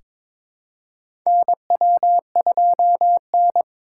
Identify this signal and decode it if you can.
Morse code, heard as a keyed tone, about 22 wpm, 700 Hz. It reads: NW2N
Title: USA Call Signs - 3x Character Spacing - 22wpm